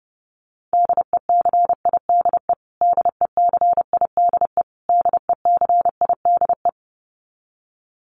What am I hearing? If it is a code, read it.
Morse code, heard as a keyed tone, about 30 wpm, 700 Hz. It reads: DECIDE DECIDE DECIDE
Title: Top 500 Words Repeated 30wpm